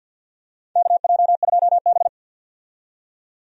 Morse code, heard as a keyed tone, about 50 wpm, 700 Hz. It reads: KJ2B
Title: USA Call Signs - 4 Characters - 50wpm